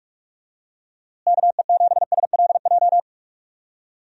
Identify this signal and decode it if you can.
Morse code, heard as a keyed tone, about 47 wpm, 700 Hz. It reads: KE7SLJ
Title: USA Call Signs - 6 Characters - Repeated 3x - 45wpm